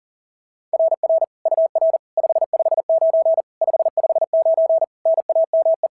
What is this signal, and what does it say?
Morse code, heard as a keyed tone, about 37 wpm, 690 Hz. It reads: RR UR 559 559 NAME
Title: SOTA QSO Practice 40wpm